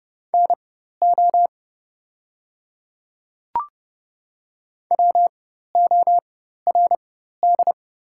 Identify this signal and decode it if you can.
Morse code, heard as a keyed tone, about 30 wpm, spaced as for 14 wpm, 700 Hz. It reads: NO WORD
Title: 4x Character Spacing - Top 100 Words - 30wpm